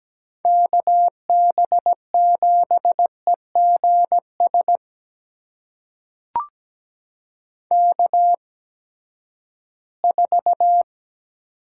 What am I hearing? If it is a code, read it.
Morse code, heard as a keyed tone, about 17 wpm, spaced as for 12 wpm, 700 Hz. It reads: KB7EGS K4
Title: USA Call Signs - 8x Character Spacing - 17wpm